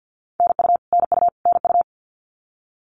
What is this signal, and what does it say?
Morse code, heard as a keyed tone, about 50 wpm, 700 Hz. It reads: NU NU NU
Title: Canadian Provinces and Territories - Repeated 3x - 50wpm